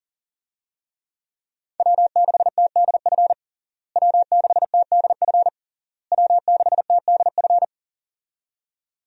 Morse code, heard as a keyed tone, about 40 wpm, 700 Hz. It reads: W6TDF W6TDF W6TDF
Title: USA Call Signs - 5 Characters - Repeated 3x - 40wpm